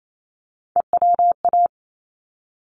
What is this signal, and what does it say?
Morse code, heard as a keyed tone, about 28 wpm, 700 Hz. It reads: EWA
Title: ARRL Sections 28wpm